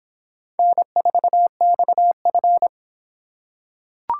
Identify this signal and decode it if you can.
Morse code, heard as a keyed tone, about 25 wpm, 700 Hz. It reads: N4XF